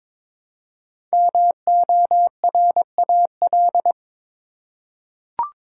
Morse code, heard as a keyed tone, about 22 wpm, 700 Hz. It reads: MORAL
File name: 5 Letter Words - First 3 Letters Given - 22wpm